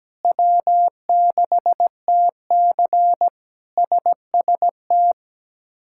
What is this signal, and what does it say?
Morse code, heard as a keyed tone, about 17 wpm, 700 Hz. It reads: W6TC SST
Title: K1USN SST QSO Practice - 17wpm